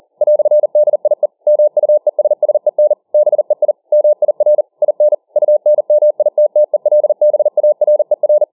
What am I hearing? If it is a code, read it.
Morse code, heard as a keyed tone, about 40 wpm, 600 Hz. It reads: "DIE MUESSEN BEI MIR IN UNMITTELBARER